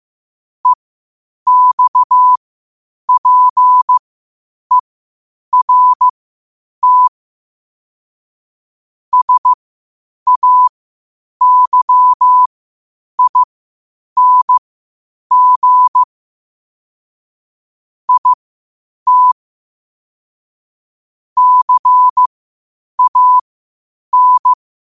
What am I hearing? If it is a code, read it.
Morse code, heard as a keyed tone, about 15 wpm, spaced as for 8 wpm, 1000 Hz. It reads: EXPERT SAYING IT CAN
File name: qotdfiles / 8wpm